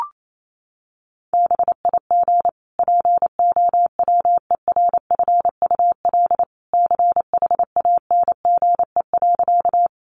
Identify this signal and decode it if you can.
Morse code, heard as a keyed tone, about 28 wpm, 700 Hz. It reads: BIG POWERFUL CHANGE.
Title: Sentences from Top 1300 Words 28wpm